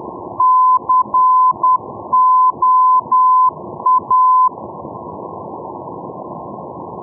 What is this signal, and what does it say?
Morse code, heard as a keyed tone, about 10 wpm, 1020 Hz. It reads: COA
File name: europe / e-601 ndb la coruna coa-401 np lacoruna 2016-05